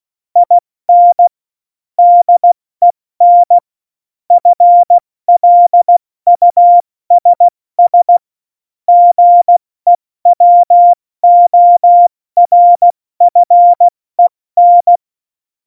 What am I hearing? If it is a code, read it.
Morse code, heard as a keyed tone, about 16 wpm, 700 Hz. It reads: IN DEN FLUSS GEWORFEN
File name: Morsekurs / Witze / Witz 24. Oktober 17, Tempo 70